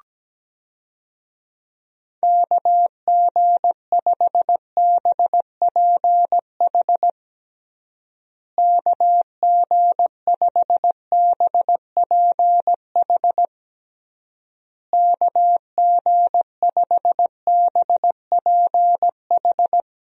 Morse code, heard as a keyed tone, about 17 wpm, 700 Hz. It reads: KG5BPH KG5BPH KG5BPH
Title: USA Call Signs - 6 Characters - Repeated 3x - 17wpm